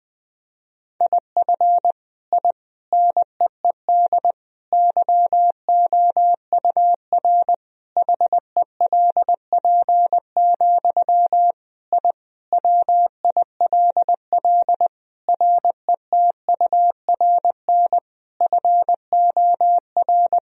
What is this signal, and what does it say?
Morse code, heard as a keyed tone, about 21 wpm, 700 Hz. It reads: IF I NEED YOUR HELP, I WILL RETURN FOR
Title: Sentences from Top 1000 Words - Review Part 1 - 20wpm